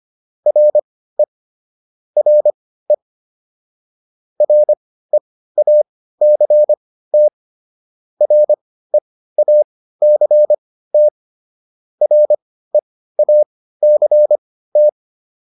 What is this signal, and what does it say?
Morse code, heard as a keyed tone, about 25 wpm, spaced as for 15 wpm, 600 Hz. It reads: RE RE REACT REACT REACT